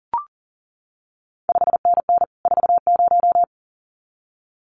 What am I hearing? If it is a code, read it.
Morse code, heard as a keyed tone, about 41 wpm, 700 Hz. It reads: 5NN 40
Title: CQ WW DX Contest Exchange Encore - 40wpm